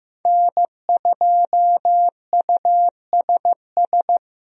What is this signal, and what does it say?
Morse code, heard as a keyed tone, about 15 wpm, 700 Hz. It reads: N2USS
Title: USA Call Signs - 5 Characters - Repeated 2x - 15wpm